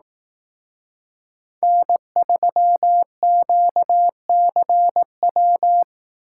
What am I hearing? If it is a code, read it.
Morse code, heard as a keyed tone, about 18 wpm, 700 Hz. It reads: N3QCW
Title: Speed Racing - USA 1x3 Call Signs 15wpm